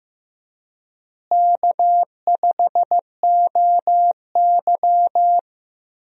Morse code, heard as a keyed tone, about 15 wpm, 700 Hz. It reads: K5OY
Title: USA Call Signs - 4 Characters - Repeated 2x - 15wpm